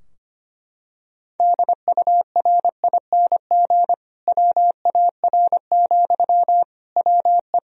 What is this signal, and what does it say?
Morse code, heard as a keyed tone, about 25 wpm, 700 Hz. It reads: DURING WAR, WE
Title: Sentences from Top 1000 Words - Review Part 4 - 25wpm